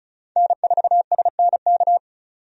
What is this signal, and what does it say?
Morse code, heard as a keyed tone, about 36 wpm, 700 Hz. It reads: N4SNK
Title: USA Call Signs - 5 Characters - 35wpm